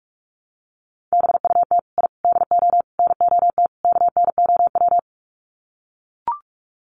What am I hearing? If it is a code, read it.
Morse code, heard as a keyed tone, about 44 wpm, 700 Hz. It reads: BUT I DO NOT KNOW
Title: Sets of 5 Words 45wpm